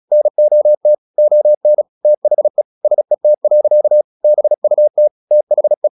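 Morse code, heard as a keyed tone, about 36 wpm, 600 Hz. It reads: NOT ON THE SET. BUT THE
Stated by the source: Morse practice at 36 wpm, text of Ken Follett's novel Never, from CW med Gnister.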